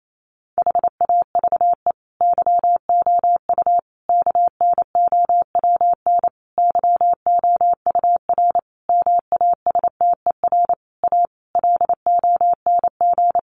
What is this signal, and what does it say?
Morse code, heard as a keyed tone, about 28 wpm, 700 Hz. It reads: HAVE YOU KNOWN YOUR MASTER A LONG
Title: Sentences from Top 1300 Words 28wpm